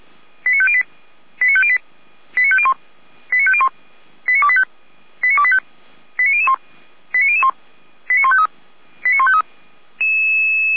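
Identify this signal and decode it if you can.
Five-tone selective call (ZVEI), heard as quick runs of five tones, tones of about 70 ms. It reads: 89508 89508 89501 89501 89285 89285 89001 89001 89153 89153
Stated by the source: ZVEI five-tone selective call received via SDR.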